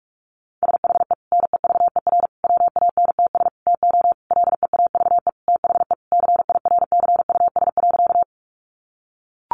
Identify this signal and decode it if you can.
Morse code, heard as a keyed tone, about 44 wpm, 700 Hz. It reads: SHE NEVER WANTS TO LEAVE THE CIRCUS.
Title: Easy Phrases 45wpm